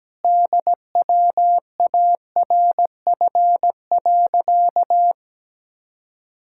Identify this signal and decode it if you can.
Morse code, heard as a keyed tone, about 17 wpm, 700 Hz. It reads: DWARF.
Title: Pangram Challenge v2 - 17wpm